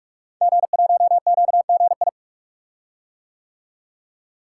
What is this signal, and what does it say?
Morse code, heard as a keyed tone, about 44 wpm, 700 Hz. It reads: G1QGI
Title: Code Groups - 5 Letter:Numbers - 45wpm